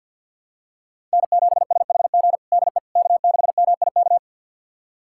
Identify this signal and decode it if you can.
Morse code, heard as a keyed tone, about 52 wpm, 700 Hz. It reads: N7SHG DE K6MIK
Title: USA Callsign DE Callsign - 5 Characters - 50wpm